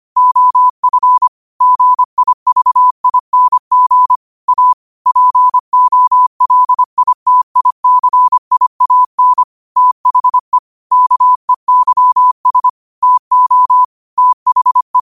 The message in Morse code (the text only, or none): OF GIVING A POLITICIAN THE KEYS TO THE